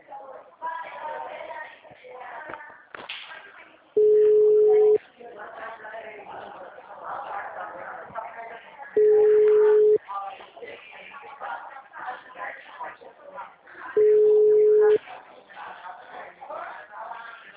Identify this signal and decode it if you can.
Ringback tone (Europe, 425 Hz), with speech.